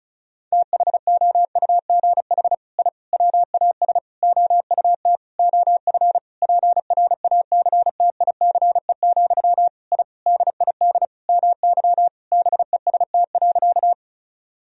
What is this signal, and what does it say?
Morse code, heard as a keyed tone, about 34 wpm, 700 Hz. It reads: THOUGH I WAS OUT OF PRACTICE, I DID MY BEST.
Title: Sentences from Top 1000 Words - Review Part 4 - 35wpm